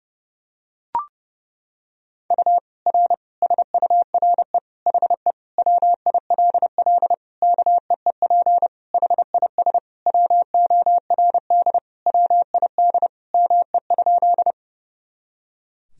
Morse code, heard as a keyed tone, about 30 wpm, 700 Hz. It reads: U R SURE HE WILL KEEP HIS WORD WID ME?
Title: Sentences from Top 300 Words and Abbreviations 30wpm